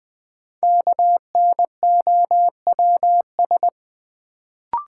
Morse code, heard as a keyed tone, about 20 wpm, 700 Hz. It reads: KNOWS
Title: Sentences from Top 1000 Words - Review Part 1 - New Word Drill - 20wpm